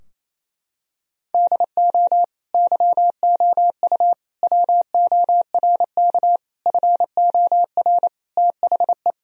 Morse code, heard as a keyed tone, about 28 wpm, 700 Hz. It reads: DO YOU WORK FOR THE
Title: Sentences from Top 1000 Words - Review Part 4 - 28wpm